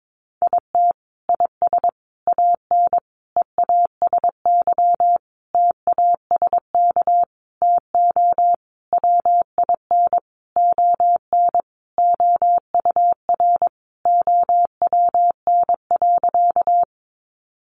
Morse code, heard as a keyed tone, about 22 wpm, 700 Hz. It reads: IT IS AN EASY TASK TO WIN ON OUR OWN.
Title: Sentences from Top 1500 Words 22wpm